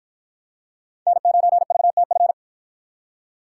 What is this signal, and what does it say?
Morse code, heard as a keyed tone, about 44 wpm, 700 Hz. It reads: N9VTF